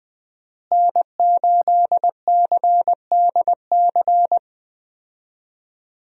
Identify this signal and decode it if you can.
Morse code, heard as a keyed tone, about 20 wpm, 700 Hz. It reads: N8CDC
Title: USA Call Signs - 5 Characters - Repeated 2x - 20wpm